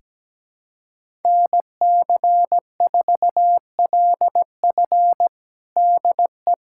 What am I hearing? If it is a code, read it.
Morse code, heard as a keyed tone, about 17 wpm, 700 Hz. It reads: NC4LF DE
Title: USA Callsign DE Callsign - 5 Characters - 17wpm